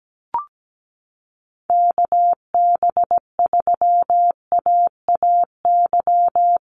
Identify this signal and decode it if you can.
Morse code, heard as a keyed tone, about 17 wpm, 700 Hz. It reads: KB3AAY
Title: Call Signs - 17wpm